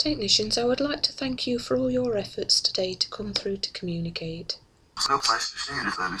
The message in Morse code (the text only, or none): none